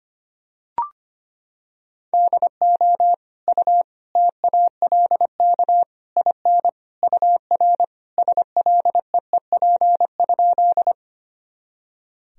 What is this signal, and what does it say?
Morse code, heard as a keyed tone, about 25 wpm, 700 Hz. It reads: DO U TALK IN UR SLEEP?
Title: Sentences from Top 500 Words and Abbreviations 25wpm